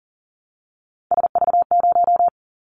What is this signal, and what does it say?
Morse code, heard as a keyed tone, about 40 wpm, 700 Hz. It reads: SV0
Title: Callsign Prefixes for Top 50 DXCC - 40wpm